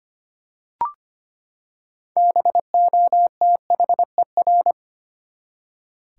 Morse code, heard as a keyed tone, about 25 wpm, 700 Hz. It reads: BOTHER
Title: Six Letter Words - 25wpm